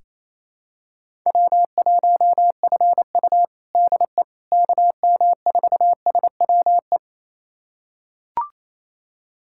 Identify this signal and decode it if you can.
Morse code, heard as a keyed tone, about 27 wpm, 700 Hz. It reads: W1FU DE KM4SWE
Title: USA Callsign DE Callsign 28wpm